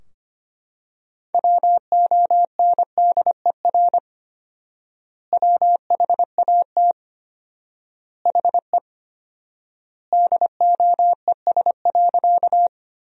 Morse code, heard as a keyed tone, about 25 wpm, 700 Hz. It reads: WONDER WHAT HE DOES.
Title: 4x Word-Spacing - Sentences from Top 500 Words - 25wpm